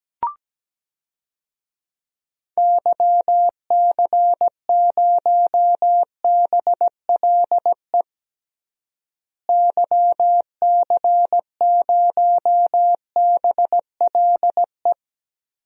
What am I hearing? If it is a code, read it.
Morse code, heard as a keyed tone, about 17 wpm, 700 Hz. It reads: YC0BLE YC0BLE
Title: DXCC-100 Call Signs - 6 Characters - Repeated 3x - 17wpm